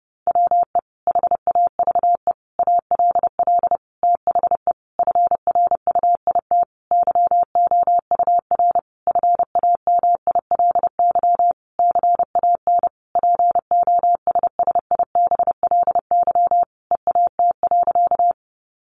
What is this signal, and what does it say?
Morse code, heard as a keyed tone, about 30 wpm, 700 Hz. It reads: WE HAVE ALL THE FRUIT YOUR FAMILY CAN POSSIBLY EAT.
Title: Sentences from Top 1300 Words 30wpm